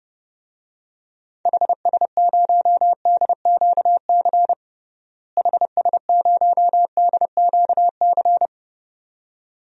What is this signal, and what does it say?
Morse code, heard as a keyed tone, about 32 wpm, 700 Hz. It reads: HS0DQC HS0DQC
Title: DXCC-100 Call Signs - 6 Characters - Repeated 2x - 30wpm